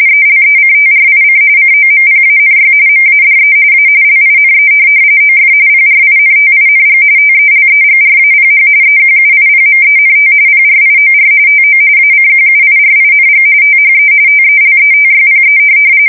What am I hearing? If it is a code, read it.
RTTY, about 45.45 baud. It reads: ELECTRIC COMPANY MADE A LINE OF PRINTING TELEGRAPH EQUIPMENT PRIOR TO ACQUIRING THE TELETYPE
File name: fsk170 / teletype-corp